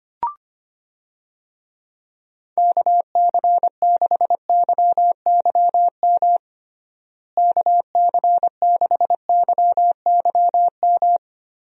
Morse code, heard as a keyed tone, about 25 wpm, 700 Hz. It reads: KC6YYM KC6YYM
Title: USA Call Signs - 6 Characters - Repeated 2x - 25wpm